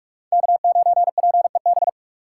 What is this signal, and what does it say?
Morse code, heard as a keyed tone, about 46 wpm, 700 Hz. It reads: K9PEB
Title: USA Call Signs - 5 Characters - 2x Character Spacing - 45wpm